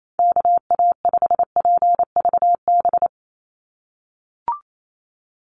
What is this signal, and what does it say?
Morse code, heard as a keyed tone, about 28 wpm, 700 Hz. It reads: KA5PVB
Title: NA SOTA Chaser Callsigns 28wpm